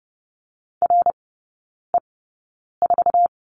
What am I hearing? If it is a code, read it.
Morse code, heard as a keyed tone, about 30 wpm, spaced as for 9 wpm, 700 Hz. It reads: RE4
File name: 081 Lesson 14e - Number 2 - Three Characters 30wpm